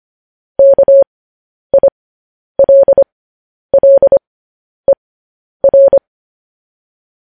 Morse code, heard as a keyed tone, about 25 wpm, spaced as for 10 wpm, 550 Hz. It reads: KILLER
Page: news-headlines-in-morse-code-at-10-wpm